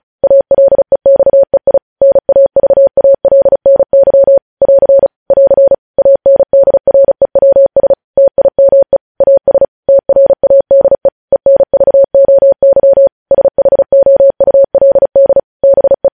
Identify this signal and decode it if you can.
Morse code, heard as a keyed tone, about 36 wpm, 550 Hz. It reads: ALEXEI NAVALNY <AR> <AR> ANDREWS TIME AS TRADE ENVOY SHOULD BE